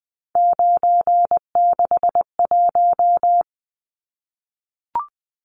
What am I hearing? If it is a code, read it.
Morse code, heard as a keyed tone, about 20 wpm, 700 Hz. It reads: 961
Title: Numbers - 3-Digit 20wpm